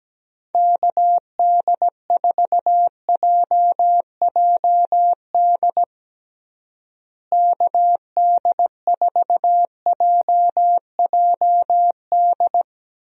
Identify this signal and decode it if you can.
Morse code, heard as a keyed tone, about 17 wpm, 700 Hz. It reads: KD4JJD KD4JJD
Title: USA Call Signs - 6 Characters - Repeated 2x - 17wpm